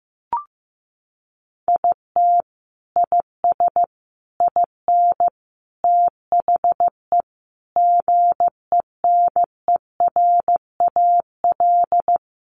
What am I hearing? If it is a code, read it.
Morse code, heard as a keyed tone, about 15 wpm, 700 Hz. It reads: IT IS IN THE GENERAL
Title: Sentences from Top 1200 Words 15wpm